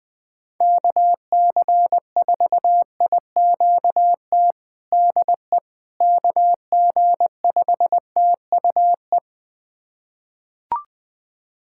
Morse code, heard as a keyed tone, about 20 wpm, 700 Hz. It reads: KC4IQT DE KG5TUE
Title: USA Callsign DE Callsign 20wpm